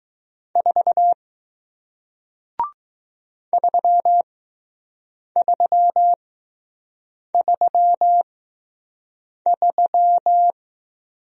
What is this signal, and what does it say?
Morse code, heard as a keyed tone, about 20 wpm, spaced as for 6 wpm, 700 Hz. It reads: 4 3333